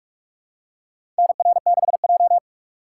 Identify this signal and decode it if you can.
Morse code, heard as a keyed tone, about 44 wpm, 700 Hz. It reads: NR6J
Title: USA Call Signs - 4 Characters - 2x Character Spacing - 45wpm